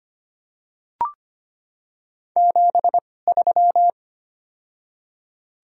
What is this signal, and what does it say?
Morse code, heard as a keyed tone, about 25 wpm, spaced as for 18 wpm, 700 Hz. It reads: 73
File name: Top 90 QSO Elements - 2x Character Spacing - 25wpm